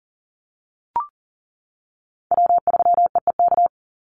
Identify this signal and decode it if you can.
Morse code, heard as a keyed tone, about 40 wpm, 700 Hz. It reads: W3EEK
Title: Call Signs - 40wpm